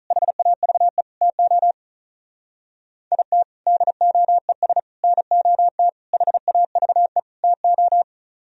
Morse code, heard as a keyed tone, about 44 wpm, 700 Hz. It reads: HAVE TO IT DOES NOT HAVE TO